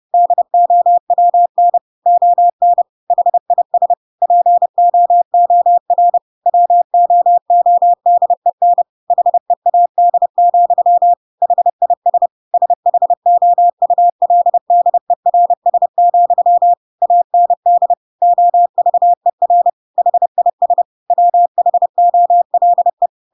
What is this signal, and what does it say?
Morse code, heard as a keyed tone, about 30 wpm, 700 Hz. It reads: DOWN ON HIS POOR WOODEN HEAD, HIS SHOULDERS, AND OVER HIS WHOLE
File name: Pinocchio / Pinocchio ch6 30 30 ik1hge